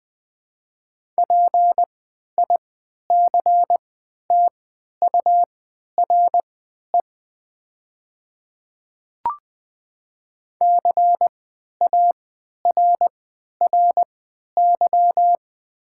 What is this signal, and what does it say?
Morse code, heard as a keyed tone, about 20 wpm, spaced as for 11 wpm, 700 Hz. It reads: PICTURE CARRY
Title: 3x Character Spacing - Top 300 Words - 20wpm